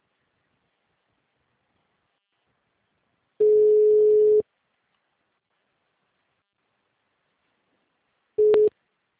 Ringback tone (Europe, 425 Hz).
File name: out- -2005- - - .244